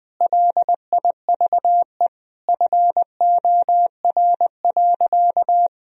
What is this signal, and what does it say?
Morse code, heard as a keyed tone, about 20 wpm, 700 Hz. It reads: LIVE FOR.
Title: Sentences from Top 1000 Words - Review Part 3 - 20wpm